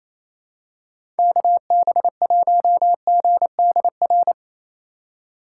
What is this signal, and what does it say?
Morse code, heard as a keyed tone, about 28 wpm, 700 Hz. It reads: KB1GDR
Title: USA Call Signs - 6 Characters - 2x Character Spacing - 28wpm